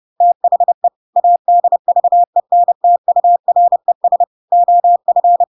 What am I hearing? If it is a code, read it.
Morse code, heard as a keyed tone, about 30 wpm, 700 Hz. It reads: THE ADVENTURES OF
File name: Pinocchio / Pinocchio ch6 30 30 ik1hge